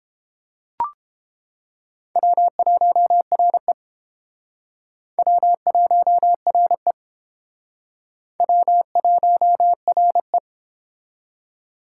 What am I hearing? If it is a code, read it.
Morse code, heard as a keyed tone, about 30 wpm, 700 Hz. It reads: W1RE W1RE W1RE